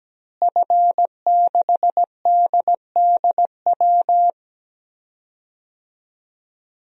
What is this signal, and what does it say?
Morse code, heard as a keyed tone, about 17 wpm, 700 Hz. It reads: F6DDW
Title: DXCC-100 Call Signs - 5 Characters - Repeated 2x - 17wpm